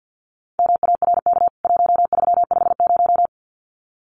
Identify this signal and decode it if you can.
Morse code, heard as a keyed tone, about 50 wpm, 700 Hz. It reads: MARK 1350
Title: CWT Exchange 50wpm